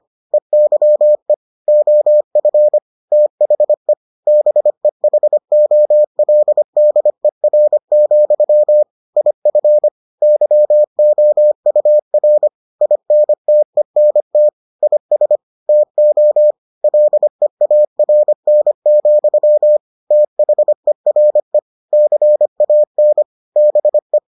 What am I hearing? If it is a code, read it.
Morse code, heard as a keyed tone, about 25 wpm, 600 Hz. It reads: EYE OF THE BEHOLDER, IF YOUR INTENT IS TO LEARN, THERE CAN BE